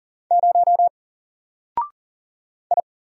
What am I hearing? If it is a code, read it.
Morse code, heard as a keyed tone, about 40 wpm, 700 Hz. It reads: 0 I
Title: Single Letter-Number Alt 40wpm